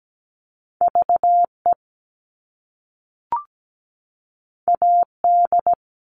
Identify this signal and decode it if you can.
Morse code, heard as a keyed tone, about 17 wpm, 700 Hz. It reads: VE AD
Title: All Cut Numbers - 2-Digit 17wpm